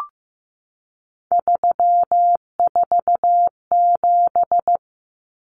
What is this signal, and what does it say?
Morse code, heard as a keyed tone, about 15 wpm, 700 Hz. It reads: 347
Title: Numbers - 3-Digit 15wpm